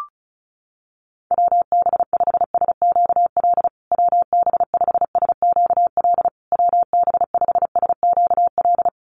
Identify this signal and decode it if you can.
Morse code, heard as a keyed tone, about 34 wpm, 700 Hz. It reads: WB5SQL WB5SQL WB5SQL
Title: USA Call Signs Repeated 3x 35wpm